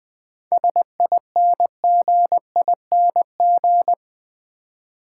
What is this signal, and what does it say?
Morse code, heard as a keyed tone, about 20 wpm, 690 Hz. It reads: SINGING
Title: 7 Letter Words - First 3 Letters Given - 20wpm